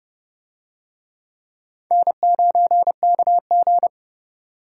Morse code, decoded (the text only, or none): N9KG